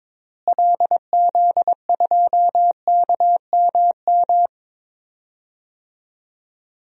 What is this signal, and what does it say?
Morse code, heard as a keyed tone, about 21 wpm, 700 Hz. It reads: LZ2KMM
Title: DXCC-100 Call Signs - 6 Characters - Repeated 2x - 22wpm